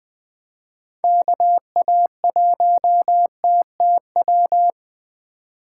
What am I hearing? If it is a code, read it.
Morse code, heard as a keyed tone, about 20 wpm, 700 Hz. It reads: KA1TTW
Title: USA Call Signs - 6 Characters - 8x Character Spacing - 20wpm